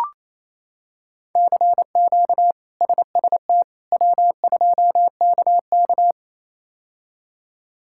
Morse code, heard as a keyed tone, about 27 wpm, 700 Hz. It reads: CQ SST W2KK
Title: K1USN SST QSO Practice - 28wpm